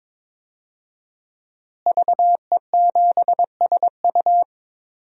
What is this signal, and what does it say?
Morse code, heard as a keyed tone, about 22 wpm, 700 Hz. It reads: VE7SU
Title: DXCC-100 Call Signs - 5 Characters - Repeated 2x - 22wpm